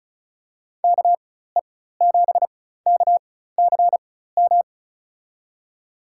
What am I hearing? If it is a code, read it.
Morse code, heard as a keyed tone, about 34 wpm, spaced as for 16 wpm, 700 Hz. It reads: KE7KCM
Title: USA Call Signs - 6 Characters - 4x Character Spacing - 35wpm